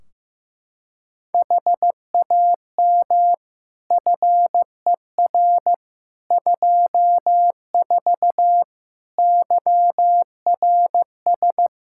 Morse code, heard as a keyed tone, about 15 wpm, 700 Hz. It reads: HAM FER 24 YRS
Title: Practice QSO - Exchange 3 - More Info - 15wpm